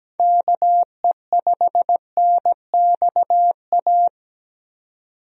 Morse code, heard as a keyed tone, about 17 wpm, 700 Hz. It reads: KE5NXA
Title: USA Call Signs - 6 Characters - 17wpm